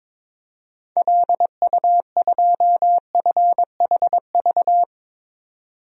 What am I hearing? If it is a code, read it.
Morse code, heard as a keyed tone, about 22 wpm, 700 Hz. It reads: LU2FHV
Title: DXCC-100 Call Signs - 6 Characters - 22wpm